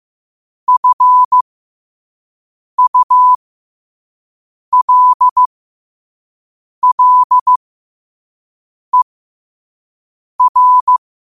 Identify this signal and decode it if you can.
Morse code, heard as a keyed tone, about 15 wpm, spaced as for 5 wpm, 1000 Hz. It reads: FULLER
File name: qotdfiles / 5wpm